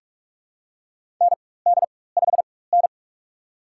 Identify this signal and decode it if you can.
Morse code, heard as a keyed tone, about 46 wpm, spaced as for 21 wpm, 700 Hz. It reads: ND5N
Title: USA Call Signs - 4 Characters - 4x Character Spacing - 45wpm